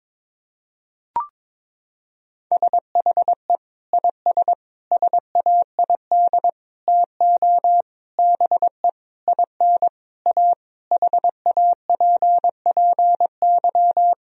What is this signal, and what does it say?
Morse code, heard as a keyed tone, about 22 wpm, 700 Hz. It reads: SHE IS SAID TO BE IN A HAPPY
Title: Sentences from Top 1000 Words - Review Part 2 - 22wpm